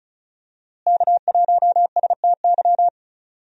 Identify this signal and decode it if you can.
Morse code, heard as a keyed tone, about 34 wpm, 700 Hz. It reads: K1STY
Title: USA Call Signs - 5 Characters - Repeated 3x - 35wpm